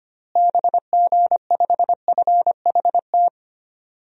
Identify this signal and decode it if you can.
Morse code, heard as a keyed tone, about 25 wpm, 700 Hz. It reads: BG5FHT
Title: DXCC-100 Call Signs - 6 Characters - Repeated 3x - 25wpm